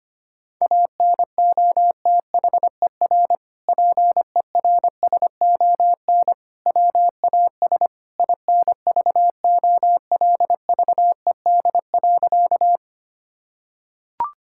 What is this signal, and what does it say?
Morse code, heard as a keyed tone, about 25 wpm, 700 Hz. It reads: ANOTHER PERSON WAS INVOLVED.
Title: Sentences from Top 1000 Words - Review Part 5 - 25wpm